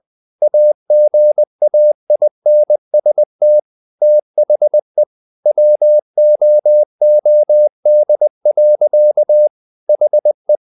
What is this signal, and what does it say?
Morse code, heard as a keyed tone, about 20 wpm, 600 Hz. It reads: AGAINST THE WOOD. HE
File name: the old man and the sea / the old man and the sea part 1 20wpm